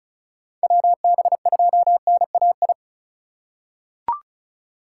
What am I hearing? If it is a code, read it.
Morse code, heard as a keyed tone, about 34 wpm, 700 Hz. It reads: WB2NAI
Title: USA Call Signs - 4x Character Spacing - 35wpm